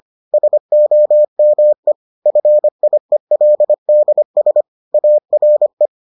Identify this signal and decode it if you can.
Morse code, heard as a keyed tone, about 25 wpm, 600 Hz. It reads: SOME FIELDS ARE